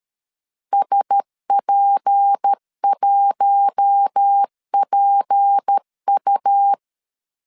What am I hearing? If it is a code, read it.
Morse code, heard as a keyed tone, about 13 wpm, 800 Hz. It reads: SP1PU
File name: LEZIONI / 6 PKX0 / 6 P 12wpm 2033